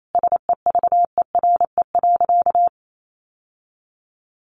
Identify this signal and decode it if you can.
Morse code, heard as a keyed tone, about 28 wpm, 700 Hz. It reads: SEVERE.
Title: Sentences from Top 1400 Words 28wpm